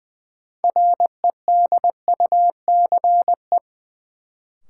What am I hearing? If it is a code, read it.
Morse code, heard as a keyed tone, about 20 wpm, 700 Hz. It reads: REDUCE
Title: Sentences from Top 1100 Words - New Word Drill Alt - 20wpm